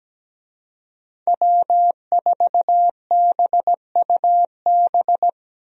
Morse code, heard as a keyed tone, about 17 wpm, 700 Hz. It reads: W4BUB
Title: USA Call Signs - 5 Characters - Repeated 3x - 17wpm